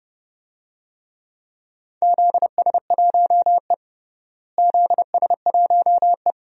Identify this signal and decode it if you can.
Morse code, heard as a keyed tone, about 29 wpm, 700 Hz. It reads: ZS1E ZS1E
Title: DXCC-100 Call Signs - 4 Characters - Repeated 3x - 30wpm